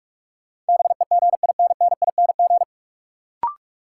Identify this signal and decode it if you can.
Morse code, heard as a keyed tone, about 46 wpm, 700 Hz. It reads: BEGINNING
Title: Sets of 3 Words - New Word Drill - 45wpm